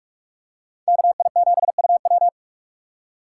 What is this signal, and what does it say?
Morse code, heard as a keyed tone, about 41 wpm, 700 Hz. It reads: KI7UW
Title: USA Call Signs - 5 Characters - 2x Character Spacing - 45wpm